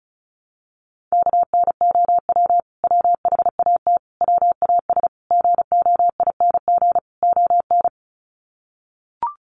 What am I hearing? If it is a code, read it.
Morse code, heard as a keyed tone, about 34 wpm, 700 Hz. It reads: KNOW WHAT WAS GOING ON
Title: Sets of 5 Words Encore2 35wpm